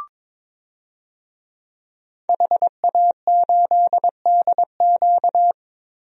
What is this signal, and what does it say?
Morse code, heard as a keyed tone, about 22 wpm, 700 Hz. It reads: HA8DQ
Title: DXCC-100 Call Signs - 5 Characters - Repeated 3x - 22wpm